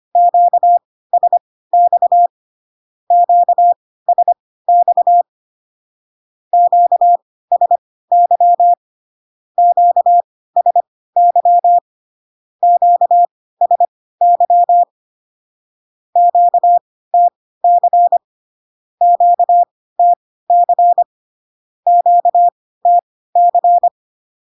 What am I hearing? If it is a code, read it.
Morse code, heard as a keyed tone, about 25 wpm, spaced as for 16 wpm, 700 Hz. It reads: QSX QSX QSY QSY QSY QTC QTC QTC
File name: cw / Q-codes most common repeated 3 times 10 25wpm ik1hge